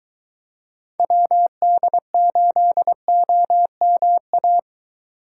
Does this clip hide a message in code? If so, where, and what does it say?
Yes: Morse code from 1.0 to 4.6 s, heard as a keyed tone, about 21 wpm, 700 Hz. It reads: WD8OMA